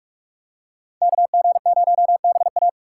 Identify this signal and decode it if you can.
Morse code, heard as a keyed tone, about 42 wpm, 700 Hz. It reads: KG0BA
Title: USA Call Signs - 5 Characters - 6x Character Spacing - 45wpm